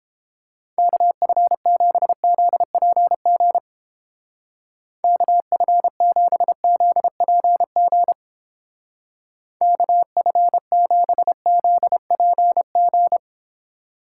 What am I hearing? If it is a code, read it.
Morse code, heard as a keyed tone, about 29 wpm, 700 Hz. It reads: KF7ZPG KF7ZPG KF7ZPG